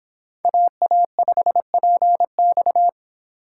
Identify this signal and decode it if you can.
Morse code, heard as a keyed tone, about 25 wpm, 700 Hz. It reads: AA5PX